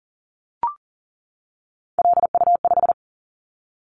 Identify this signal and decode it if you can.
Morse code, heard as a keyed tone, about 40 wpm, 700 Hz. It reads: LU5
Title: Callsign Prefixes for Top 100 DXCC - 40wpm